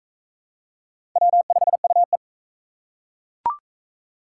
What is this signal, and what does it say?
Morse code, heard as a keyed tone, about 37 wpm, 700 Hz. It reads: W5UE